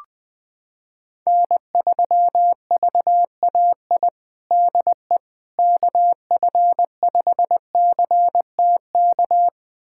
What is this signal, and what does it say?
Morse code, heard as a keyed tone, about 20 wpm, 700 Hz. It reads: N3VAI DE KF5CTK
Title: USA Callsign DE Callsign 20wpm